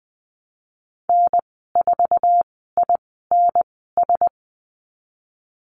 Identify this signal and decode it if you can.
Morse code, heard as a keyed tone, about 20 wpm, spaced as for 14 wpm, 700 Hz. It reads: N4INS
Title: 082 Lesson 14g - Number 2 - Call Signs 20wpm